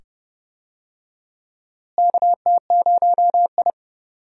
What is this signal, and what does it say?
Morse code, heard as a keyed tone, about 30 wpm, 700 Hz. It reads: KT0I
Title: USA Call Signs - 4 Characters - Repeated 2x - 30wpm